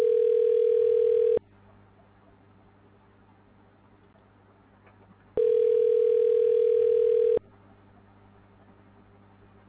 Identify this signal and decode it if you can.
Ringback tone (North America), 440 + 480 Hz.